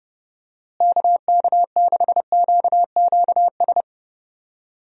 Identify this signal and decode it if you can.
Morse code, heard as a keyed tone, about 32 wpm, 700 Hz. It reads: KK6QQS
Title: USA Call Signs - 6 Characters - Repeated 2x - 30wpm